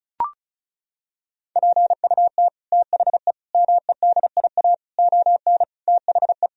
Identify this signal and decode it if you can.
Morse code, heard as a keyed tone, about 36 wpm, 700 Hz. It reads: PUT THE MEDIA ON THE
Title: Sentences from Top 1000 Words - Review Part 4 - 35wpm